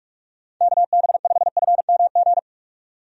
Morse code, heard as a keyed tone, about 45 wpm, 700 Hz. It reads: KB5FMG
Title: USA Call Signs - 6 Characters - 45wpm